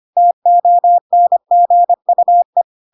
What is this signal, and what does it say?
Morse code, heard as a keyed tone, about 25 wpm, 700 Hz. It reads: TONGUE
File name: Pinocchio / Pinocchio ch1 25 25 ik1hge